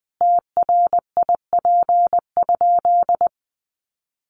Morse code, heard as a keyed tone, about 20 wpm, 700 Hz. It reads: TRIP?
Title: Sentences from Top 1300 Words 20wpm